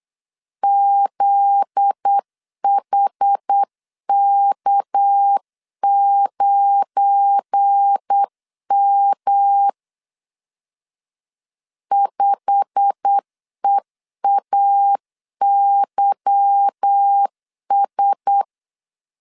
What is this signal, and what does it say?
Morse code, heard as a keyed tone, about 8 wpm, 800 Hz. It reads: ZHK9M 5EAYS
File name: CW / DET / 1 9 1500C S A 8wpm 5637